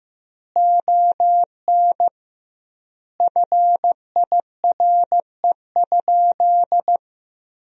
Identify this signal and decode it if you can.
Morse code, heard as a keyed tone, about 16 wpm, 700 Hz. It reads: ON FIRE?
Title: 2x Word-Spacing - Sentences from Top 400 Words - 15wpm